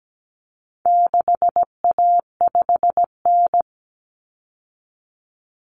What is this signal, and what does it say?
Morse code, heard as a keyed tone, about 17 wpm, 700 Hz. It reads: 6A5N
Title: Common Cut Numbers - 4-Digit 17wpm